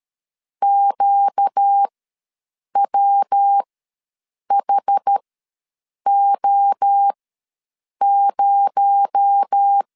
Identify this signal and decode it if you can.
Morse code, heard as a keyed tone, about 13 wpm, spaced as for 7 wpm, 800 Hz. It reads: QWHO0
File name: LEZIONI / 6 PKX0 / 6 PKX0 ws 12wpm 3448